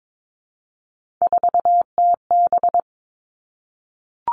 Morse code, heard as a keyed tone, about 22 wpm, 700 Hz. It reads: 4TB
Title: All Cut Numbers - 3-Digit 22wpm